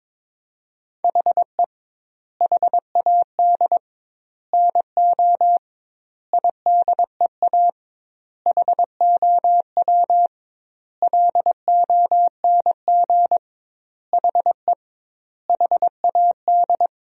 Morse code, heard as a keyed tone, about 22 wpm, 700 Hz. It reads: HE HAD NO IDEA HOW LONG HE HAD
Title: 2x Word-Spacing - Sentences from Top 300 Words - 22wpm